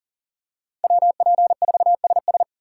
Morse code, heard as a keyed tone, about 40 wpm, 700 Hz. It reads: WP4SS
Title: USA Call Signs - 5 Characters - 2x Character Spacing - 40wpm